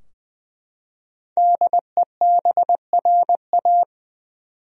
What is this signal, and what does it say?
Morse code, heard as a keyed tone, about 20 wpm, 700 Hz. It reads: DEBRA
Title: USA Common Names - 20wpm